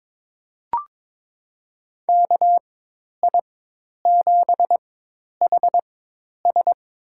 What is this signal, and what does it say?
Morse code, heard as a keyed tone, about 21 wpm, spaced as for 10 wpm, 700 Hz. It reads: KI7HS
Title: USA Call Signs - 4x Character Spacing - 22wpm